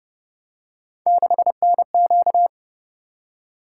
Morse code, heard as a keyed tone, about 29 wpm, 700 Hz. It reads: 6NQ